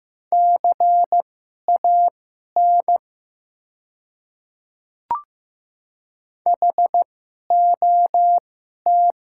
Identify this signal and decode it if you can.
Morse code, heard as a keyed tone, about 15 wpm, spaced as for 11 wpm, 700 Hz. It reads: CAN HOT
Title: 2x Character Spacing - Top 100 Words - 15wpm